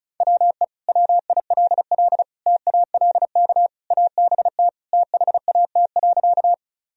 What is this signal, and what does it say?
Morse code, heard as a keyed tone, about 36 wpm, 700 Hz. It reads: WE WILL TALK ABT THAT.
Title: Sentences from Top 400 Words and Abbreviations 35wpm